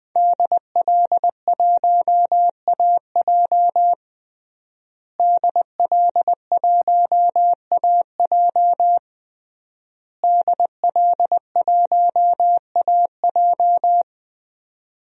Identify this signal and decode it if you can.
Morse code, heard as a keyed tone, about 20 wpm, 700 Hz. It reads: DL1AJ DL1AJ DL1AJ
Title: DXCC-100 Call Signs - 5 Characters - Repeated 3x - 20wpm